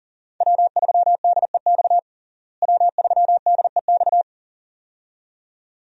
Morse code, heard as a keyed tone, about 37 wpm, 700 Hz. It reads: W3DEX W3DEX
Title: USA Call Signs - 5 Characters - Repeated 2x - 40wpm